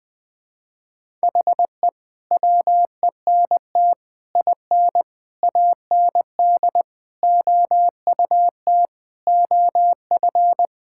Morse code, heard as a keyed tone, about 20 wpm, 700 Hz. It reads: HE WENT IN AND OUT OF
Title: Sentences with 5-Word Phrases - Spoken Beginning - 20wpm